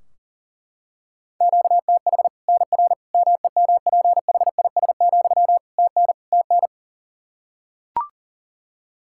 Morse code, heard as a keyed tone, about 40 wpm, 700 Hz. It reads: QTH NR MEMPHIS, TN TN
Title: Practice QSO - Exchange 1 - Beginning - 40wpm